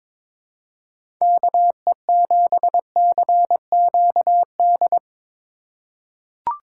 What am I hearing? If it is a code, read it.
Morse code, heard as a keyed tone, about 22 wpm, 700 Hz. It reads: KE7CQD
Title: USA Call Signs - 6 Characters - 3x Character Spacing - 22wpm